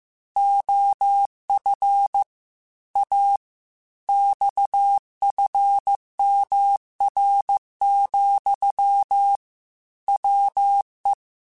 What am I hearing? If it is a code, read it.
Morse code, heard as a keyed tone, about 15 wpm, 800 Hz. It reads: OF A XFMR, WE
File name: Technical Subjects / Technical Subjects-mp3s / Sw Pwr Sup @13